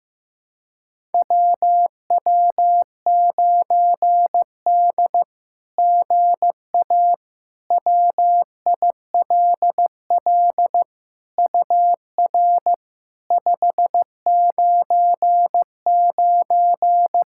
Morse code, heard as a keyed tone, about 15 wpm, 700 Hz. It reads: WW9D GA WILL UR 599
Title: SOTA QSO Practice 15wpm